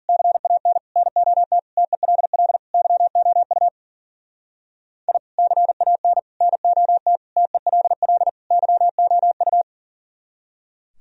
Morse code, heard as a keyed tone, about 48 wpm, 700 Hz. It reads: CAN NOT TELL YOU I CAN NOT TELL YOU
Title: Speed-Racing - Sets of 5 Words 40wpm